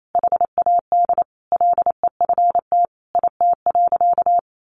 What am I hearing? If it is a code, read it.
Morse code, heard as a keyed tone, about 28 wpm, 700 Hz. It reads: HAD LEFT IT.
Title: A Sound Of Thunder 28wpm